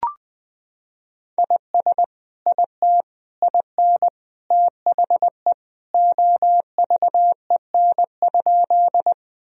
Morse code, heard as a keyed tone, about 21 wpm, 700 Hz. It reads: IS IT IN THE OVEN?
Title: Sentences from Top 1000 Words - Review Part 4 - 20wpm